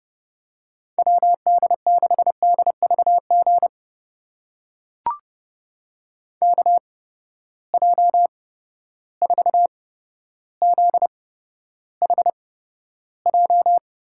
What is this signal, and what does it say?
Morse code, heard as a keyed tone, about 30 wpm, spaced as for 13 wpm, 700 Hz. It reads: WD6DVG KJ4ZHJ
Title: USA Call Signs - 6 Characters - 8x Character Spacing - 30wpm